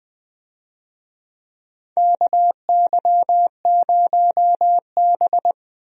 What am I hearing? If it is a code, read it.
Morse code, heard as a keyed tone, about 20 wpm, 700 Hz. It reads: KY0B
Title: USA Call Signs - 4 Characters - Repeated 3x - 20wpm